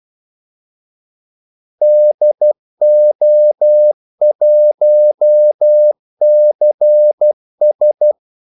Morse code, heard as a keyed tone, about 12 wpm, 600 Hz. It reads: DO1CS
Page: 2 ekstra ordmellemrum mellem kaldesignalerne